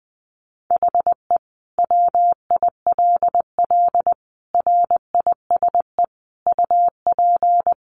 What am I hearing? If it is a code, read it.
Morse code, heard as a keyed tone, about 20 wpm, 700 Hz. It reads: HE WILL RISE UP
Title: Sentences from Top 900 Words 20wpm